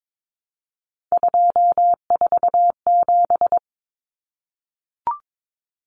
Morse code, heard as a keyed tone, about 22 wpm, 700 Hz. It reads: 247
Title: Numbers - 3-Digit 22wpm